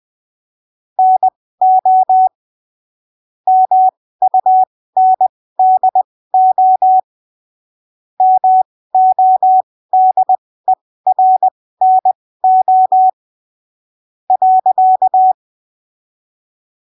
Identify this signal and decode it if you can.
Morse code, heard as a keyed tone, about 20 wpm, spaced as for 15 wpm, 750 Hz. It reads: NO MUNDO MODERNO .
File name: arquivos / texto / texto 24 20ppm